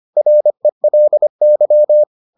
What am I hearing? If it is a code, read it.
Morse code, heard as a keyed tone, about 25 wpm, 600 Hz. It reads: RELY